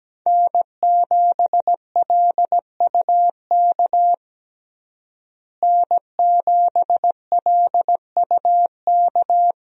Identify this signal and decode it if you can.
Morse code, heard as a keyed tone, about 17 wpm, 700 Hz. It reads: N7LUK N7LUK
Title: USA Call Signs - 5 Characters - Repeated 3x - 17wpm